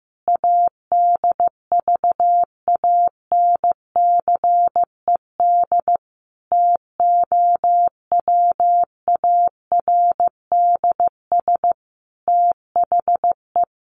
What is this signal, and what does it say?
Morse code, heard as a keyed tone, about 15 wpm, 700 Hz. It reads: ADVANCED TOWARDS THE
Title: Sentences from Top 1200 Words 15wpm